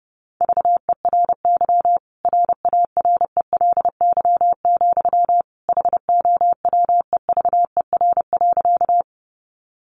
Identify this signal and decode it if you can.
Morse code, heard as a keyed tone, about 30 wpm, 700 Hz. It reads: VERY RARELY, HOWEVER.
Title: Sentences from Top 1400 Words 30wpm